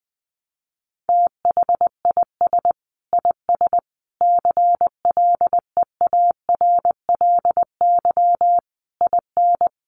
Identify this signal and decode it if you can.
Morse code, heard as a keyed tone, about 20 wpm, 700 Hz. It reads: THIS IS CLEARLY IN
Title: Sentences from Top 1300 Words 20wpm